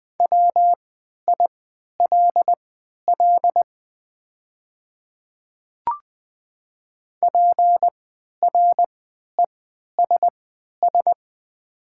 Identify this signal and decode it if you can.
Morse code, heard as a keyed tone, about 20 wpm, spaced as for 11 wpm, 690 Hz. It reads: WILL PRESS
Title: 3x Character Spacing - Top 500 Words - 20wpm